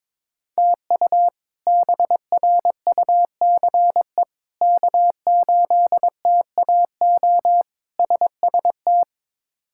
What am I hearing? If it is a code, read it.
Morse code, heard as a keyed tone, about 22 wpm, 700 Hz. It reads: TU BRUCE K8TAO SST
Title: K1USN SST QSO Practice - 22wpm